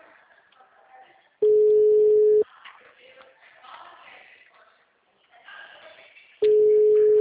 Ringback tone (Europe, 425 Hz).